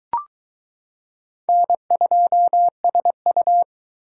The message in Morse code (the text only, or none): N2SU